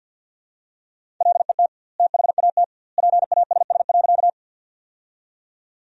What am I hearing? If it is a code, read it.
Morse code, heard as a keyed tone, about 52 wpm, 700 Hz. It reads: LET THAT PASS.
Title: 2x Word-Spacing - Sentences from Top 400 Words - 50wpm